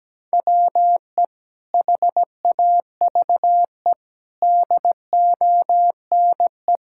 Morse code, heard as a keyed tone, about 17 wpm, 700 Hz. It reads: WE HAVE DONE
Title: Sentences from Top 1000 Words - Review Part 5 - 17wpm